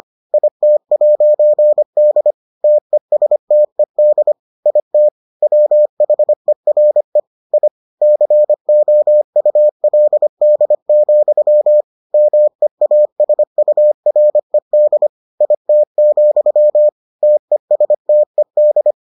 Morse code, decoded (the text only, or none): IT'D TESTED IT WHERE I COULD, MEASURED IT, TESTED